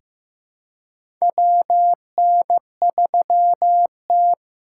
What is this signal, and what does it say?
Morse code, heard as a keyed tone, about 15 wpm, 700 Hz. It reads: WN3T
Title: USA Call Signs - 4 Characters - 4x Character Spacing - 15wpm